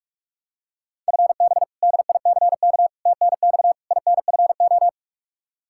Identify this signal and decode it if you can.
Morse code, heard as a keyed tone, about 46 wpm, 700 Hz. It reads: FB DICK TNX INFO
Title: Practice QSO - Exchange 3 - More Info - 45wpm